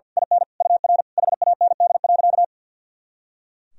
Morse code, heard as a keyed tone, about 46 wpm, 700 Hz. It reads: IN UR HAND.
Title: Sentences from Top 400 Words and Abbreviations 50wpm